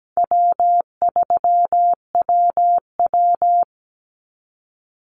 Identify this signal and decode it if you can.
Morse code, heard as a keyed tone, about 17 wpm, 700 Hz. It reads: W3WW
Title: Easy Call Signs - 17wpm